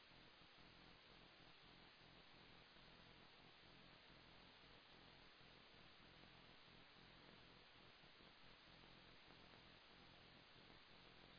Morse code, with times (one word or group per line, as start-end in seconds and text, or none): none